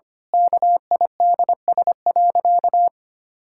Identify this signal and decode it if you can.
Morse code, heard as a keyed tone, about 25 wpm, 700 Hz. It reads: KIDS.
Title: Pangram Challenge v2 - 25wpm